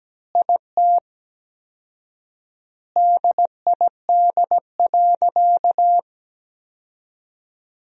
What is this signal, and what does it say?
Morse code, heard as a keyed tone, about 17 wpm, 700 Hz. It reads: IT DID.
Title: 4x Word-Spacing - Sentences from Top 500 Words - 17wpm